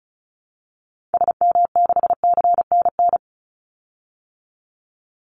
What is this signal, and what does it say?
Morse code, heard as a keyed tone, about 34 wpm, 700 Hz. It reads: SM6CNN
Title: Call Signs Repeated 2x - 35wpm